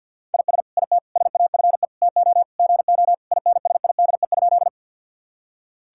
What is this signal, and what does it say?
Morse code, heard as a keyed tone, about 50 wpm, 700 Hz. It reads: IS IT SAFE TO GO INSIDE?
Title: Sentences from Top 1000 Words - Review Part 6 - 50wpm